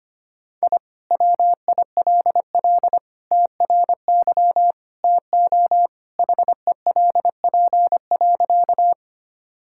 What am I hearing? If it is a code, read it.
Morse code, heard as a keyed tone, about 25 wpm, 700 Hz. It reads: I WILL TRY TO HELP.
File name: practice / 25wpm / Sentences with 3-Word Phrases - 25wpm